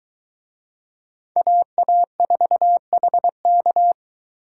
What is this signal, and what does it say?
Morse code, heard as a keyed tone, about 24 wpm, 700 Hz. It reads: AA4HK